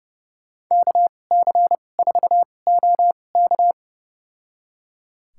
Morse code, heard as a keyed tone, about 30 wpm, spaced as for 22 wpm, 700 Hz. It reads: KC4OK
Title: USA Call Signs - 5 Characters - 2x Character Spacing - 30wpm